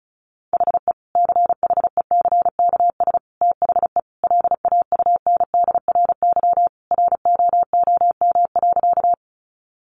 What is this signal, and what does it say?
Morse code, heard as a keyed tone, about 34 wpm, 700 Hz. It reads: HE CHECKS THE LAUNDRY ROOM.
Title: Easy Phrases 35wpm